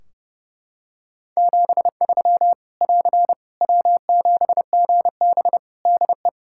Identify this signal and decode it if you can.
Morse code, heard as a keyed tone, about 32 wpm, 700 Hz. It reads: 73 <AR> W7GB DE
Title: Practice QSO - Exchange 4 - Ending - 30wpm